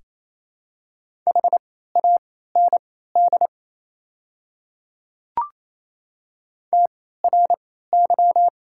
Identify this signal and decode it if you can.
Morse code, heard as a keyed tone, about 28 wpm, spaced as for 16 wpm, 700 Hz. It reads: HAND TRY
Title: 3x Character Spacing - Top 300 Words - 28wpm